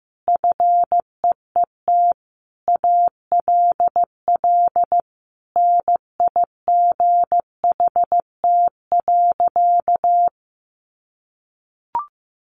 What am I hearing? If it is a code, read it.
Morse code, heard as a keyed tone, about 15 wpm, 700 Hz. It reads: FEET ALL NIGHT.
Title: Sentences from Top 1500 Words 15wpm